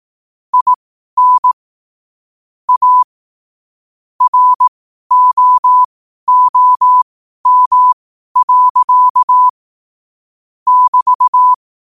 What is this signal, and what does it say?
Morse code, heard as a keyed tone, about 18 wpm, spaced as for 12 wpm, 1000 Hz. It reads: IN A ROOM. <BT>
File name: qotdfiles / 12wpm